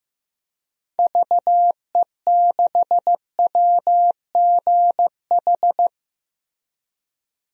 Morse code, heard as a keyed tone, about 15 wpm, 700 Hz. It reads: VE6WGH
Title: DXCC-100 Call Signs - 6 Characters - Repeated 2x - 15wpm